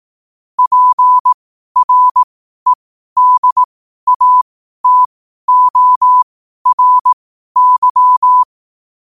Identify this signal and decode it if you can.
Morse code, heard as a keyed tone, about 18 wpm, spaced as for 12 wpm, 1000 Hz. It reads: PREDATORY
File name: qotdfiles / 12wpm